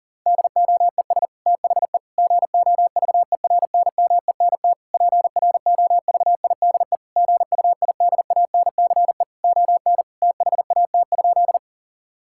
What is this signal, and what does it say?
Morse code, heard as a keyed tone, about 40 wpm, 700 Hz. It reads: DOES THE GOVERNMENT PROVIDE GUIDANCE ON THAT?
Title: Sentences from Top 1600 Words 40wpm